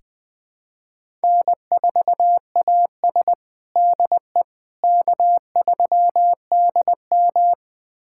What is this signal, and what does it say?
Morse code, heard as a keyed tone, about 20 wpm, 700 Hz. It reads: N4AS DE K3DM
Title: USA Callsign DE Callsign - 4 Characters - 20wpm